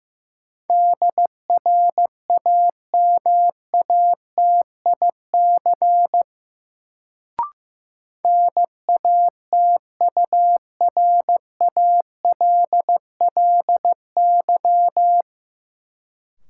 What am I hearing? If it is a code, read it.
Morse code, heard as a keyed tone, about 15 wpm, 700 Hz. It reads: DRAMATIC NATURALLY
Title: Sentences from Top 1300 Words - New Word Drill Alt - 15wpm